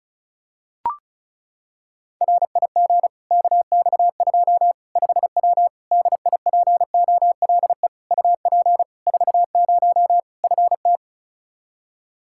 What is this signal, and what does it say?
Morse code, heard as a keyed tone, about 34 wpm, 700 Hz. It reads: RIG KX2 5W DIPOLE UP 40 FT
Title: Practice QSO - Exchange 2 - Info - 35wpm